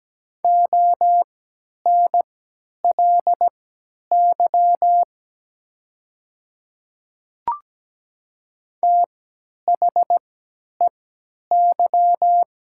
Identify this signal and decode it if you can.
Morse code, heard as a keyed tone, about 17 wpm, spaced as for 10 wpm, 700 Hz. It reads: ONLY THEY
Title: 3x Character Spacing - Top 500 Words - 17wpm